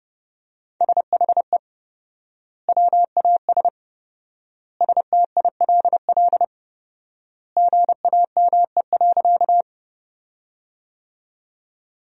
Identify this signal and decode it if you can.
Morse code, heard as a keyed tone, about 31 wpm, 700 Hz. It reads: SHE WAS STILL GAME.
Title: 4x Word-Spacing - Sentences from Top 500 Words - 30wpm